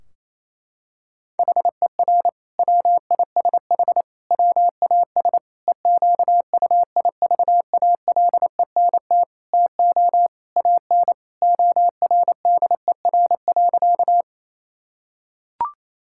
Morse code, heard as a keyed tone, about 28 wpm, 700 Hz. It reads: HER WISH WAS EQUIVALENT TO AN ORDER.
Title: Sentences from Top 1600 Words 28wpm